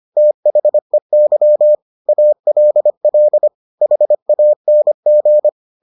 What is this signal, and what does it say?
Morse code, heard as a keyed tone, about 25 wpm, 600 Hz. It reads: THEY ALL HANG